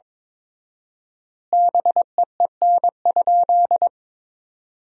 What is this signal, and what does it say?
Morse code, heard as a keyed tone, about 22 wpm, 700 Hz. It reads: BEEN?
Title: 4x Word-Spacing - Sentences from Top 200 Words - 22wpm